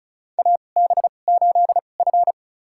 Morse code, heard as a keyed tone, about 34 wpm, spaced as for 25 wpm, 700 Hz. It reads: AB8F
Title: USA Call Signs - 4 Characters - 2x Character Spacing - 35wpm